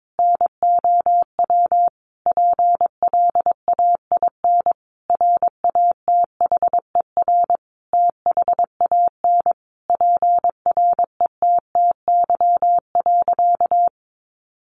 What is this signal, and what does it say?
Morse code, heard as a keyed tone, about 22 wpm, 700 Hz. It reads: NOW PLAIN RATHER THAN PRETTY.
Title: Sentences from Top 1000 Words 22wpm